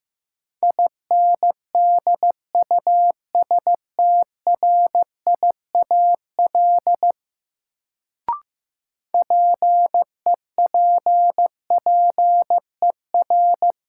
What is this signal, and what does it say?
Morse code, heard as a keyed tone, about 15 wpm, 700 Hz. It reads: INDUSTRIAL PEPPER
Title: Sets of 5 Words - New Word Drill - 15wpm